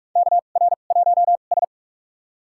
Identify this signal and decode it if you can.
Morse code, heard as a keyed tone, about 46 wpm, spaced as for 32 wpm, 700 Hz. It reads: KR1S
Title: USA Call Signs - 2x Character Spacing - 45wpm